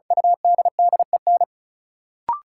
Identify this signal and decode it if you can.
Morse code, heard as a keyed tone, about 36 wpm, 700 Hz. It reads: UDDEN
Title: Sets of 4 Words - New Word Drill - 35wpm